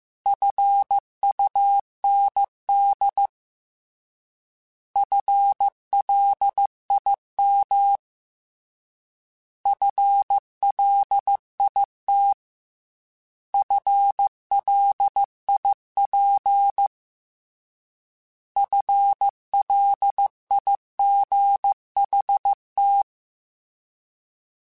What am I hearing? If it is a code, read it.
Morse code, heard as a keyed tone, about 15 wpm, 800 Hz. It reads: FUND FLIM FLIT FLIP FLIGHT
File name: Alphabetical / Alphabetical mp3s / Fs @13